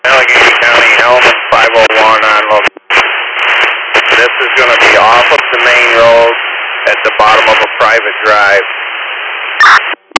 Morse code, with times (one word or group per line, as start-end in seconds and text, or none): none